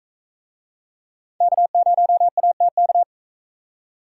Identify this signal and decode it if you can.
Morse code, heard as a keyed tone, about 44 wpm, 700 Hz. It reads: K0ATK